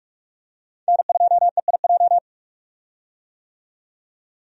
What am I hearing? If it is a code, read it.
Morse code, heard as a keyed tone, about 46 wpm, 700 Hz. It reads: N2EIJ
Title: USA Call Signs - 5 Characters - 45wpm